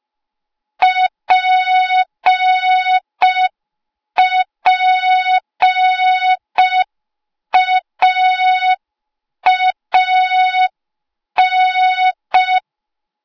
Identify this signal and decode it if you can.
Morse code, heard as a keyed tone, about 5 wpm, 3700 Hz. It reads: PPAAN
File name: Morze-HA2MN / hang / 08KXC HA2MN